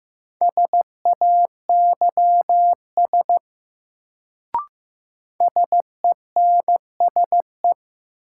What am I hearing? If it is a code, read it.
Morse code, heard as a keyed tone, about 15 wpm, 700 Hz. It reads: SAYS SENSE
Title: Sets of 2 Words - New Word Drill - 15wpm